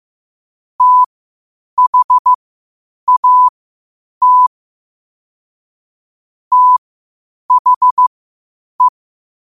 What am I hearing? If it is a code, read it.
Morse code, heard as a keyed tone, about 15 wpm, spaced as for 8 wpm, 1000 Hz. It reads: THAT THE
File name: qotdfiles / 8wpm